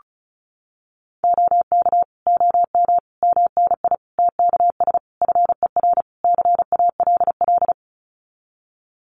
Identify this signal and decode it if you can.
Morse code, heard as a keyed tone, about 34 wpm, 700 Hz. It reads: OK OM MNI TKS FER CALL
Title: Common QSO Phrases 35wpm